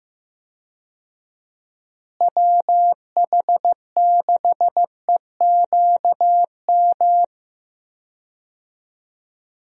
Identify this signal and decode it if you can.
Morse code, heard as a keyed tone, about 15 wpm, 700 Hz. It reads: WH6EQM
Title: USA Call Signs - 6 Characters - Repeated 2x - 15wpm